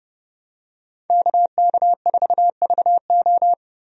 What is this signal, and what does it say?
Morse code, heard as a keyed tone, about 30 wpm, 700 Hz. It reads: KK4VO
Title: USA Call Signs - 5 Characters - Repeated 2x - 30wpm